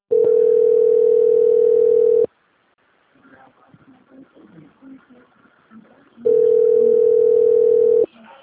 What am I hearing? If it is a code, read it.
Ringback tone (North America), 440 + 480 Hz.